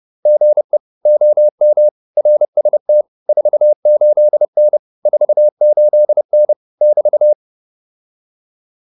Morse code, heard as a keyed tone, about 30 wpm, 600 Hz. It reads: GE OM RST 48N 48N <BT>